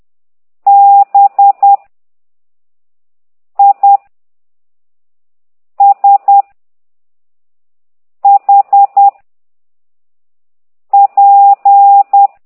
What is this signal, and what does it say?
Morse code, heard as a keyed tone, about 10 wpm, 800 Hz. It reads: B I S H P
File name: ct2idw / exerc9